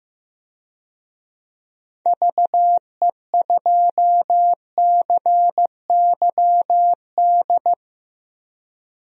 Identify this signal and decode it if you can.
Morse code, heard as a keyed tone, about 15 wpm, 700 Hz. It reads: VE2CYD
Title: DXCC-100 Call Signs - 6 Characters - Repeated 2x - 15wpm